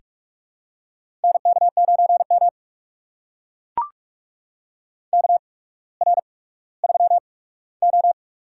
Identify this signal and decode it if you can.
Morse code, heard as a keyed tone, about 44 wpm, spaced as for 19 wpm, 690 Hz. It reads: NK9M KR3O
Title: USA Call Signs - 4 Characters - 8x Character Spacing - 45wpm